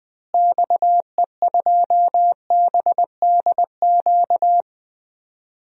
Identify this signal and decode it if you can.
Morse code, heard as a keyed tone, about 20 wpm, 700 Hz. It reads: XE2BDQ
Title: DXCC-100 Call Signs - 6 Characters - Repeated 2x - 20wpm